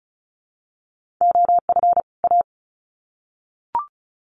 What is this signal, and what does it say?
Morse code, heard as a keyed tone, about 34 wpm, 700 Hz. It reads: OF A
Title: Sets of 2 Words 35wpm